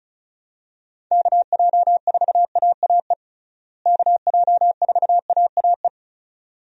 Morse code, heard as a keyed tone, about 36 wpm, 700 Hz. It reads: KJ4AAE KJ4AAE
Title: USA Call Signs - 6 Characters - Repeated 2x - 35wpm